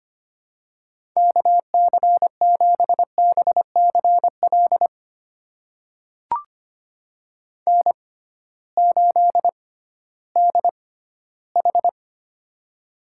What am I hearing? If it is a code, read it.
Morse code, heard as a keyed tone, about 25 wpm, spaced as for 14 wpm, 700 Hz. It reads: KC7BCL N8DH
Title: USA Call Signs - 6x Character Spacing - 25wpm